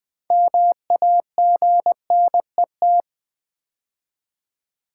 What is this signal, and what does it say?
Morse code, heard as a keyed tone, about 20 wpm, 700 Hz. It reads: MAGNET
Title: Build A Word - 2 Letters At A Time - 8-Letter Words - 20wpm